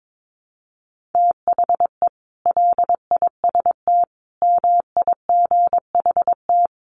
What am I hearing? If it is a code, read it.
Morse code, heard as a keyed tone, about 22 wpm, 700 Hz. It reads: THE LIST MIGHT
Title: Sentences from Top 1100 Words 22wpm